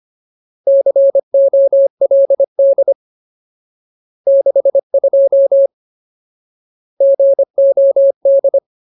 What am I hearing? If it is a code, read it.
Morse code, heard as a keyed tone, about 25 wpm, 550 Hz. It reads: COLD 62 GOD
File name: Ultimate Word List / 25WPM Ultimate